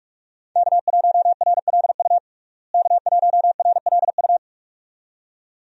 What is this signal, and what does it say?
Morse code, heard as a keyed tone, about 46 wpm, 700 Hz. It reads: K1RLU K1RLU
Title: USA Call Signs - 5 Characters - Repeated 2x - 45wpm